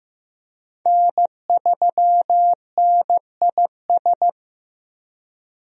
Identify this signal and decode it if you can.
Morse code, heard as a keyed tone, about 15 wpm, 700 Hz. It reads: N3NIS
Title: USA Call Signs - 5 Characters - Repeated 3x - 15wpm